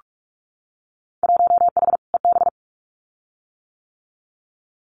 Morse code, heard as a keyed tone, about 46 wpm, 700 Hz. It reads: 1H EB
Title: Winter Field Day Exchanges 45wpm